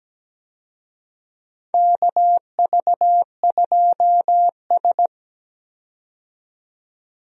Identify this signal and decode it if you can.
Morse code, heard as a keyed tone, about 17 wpm, 700 Hz. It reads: KV2S
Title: USA Call Signs - 4 Characters - 17wpm